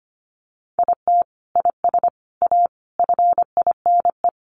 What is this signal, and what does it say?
Morse code, heard as a keyed tone, about 25 wpm, 700 Hz. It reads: IT IS A FINE
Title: Sentences from Top 1000 Words 25wpm